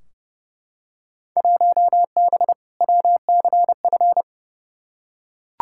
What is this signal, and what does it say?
Morse code, heard as a keyed tone, about 30 wpm, 700 Hz. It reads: 1B WCF
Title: Field Day Exchanges Encore - 30wpm